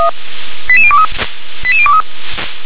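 Five-tone selective call (ZVEI), heard as quick runs of five tones, tones of about 70 ms. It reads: 88023 88023
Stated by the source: Five-tone ZVEI selective call 88163 received by SDR.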